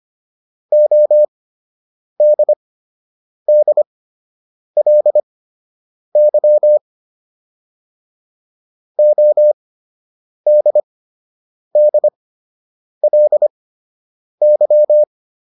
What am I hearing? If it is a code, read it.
Morse code, heard as a keyed tone, about 25 wpm, spaced as for 8 wpm, 600 Hz. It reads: ODDLY ODDLY